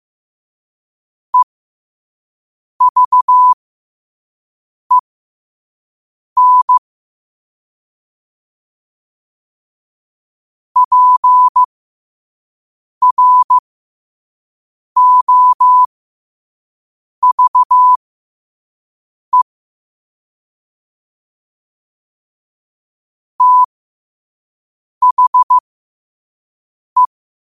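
Morse code, heard as a keyed tone, about 15 wpm, spaced as for 5 wpm, 1000 Hz. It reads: EVEN PROVE THE
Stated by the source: Morse code quote-of-the-day practice at 5 wpm.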